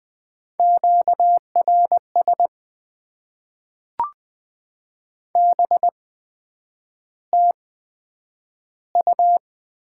Morse code, heard as a keyed tone, about 20 wpm, spaced as for 9 wpm, 700 Hz. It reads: QRS BTU
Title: Top 90 QSO Elements - 8x Character Spacing - 20wpm